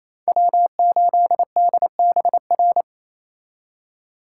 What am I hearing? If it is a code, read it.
Morse code, heard as a keyed tone, about 36 wpm, 700 Hz. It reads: W8DBR